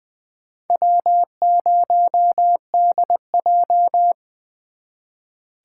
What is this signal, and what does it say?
Morse code, heard as a keyed tone, about 24 wpm, 700 Hz. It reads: W0DJ
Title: Speed Racing - USA 1x2 Call Signs 20wpm